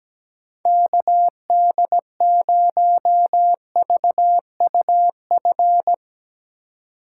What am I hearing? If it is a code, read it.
Morse code, heard as a keyed tone, about 17 wpm, 700 Hz. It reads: KD0VUF
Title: Speed Racing - USA 2x3 Call Signs 17wpm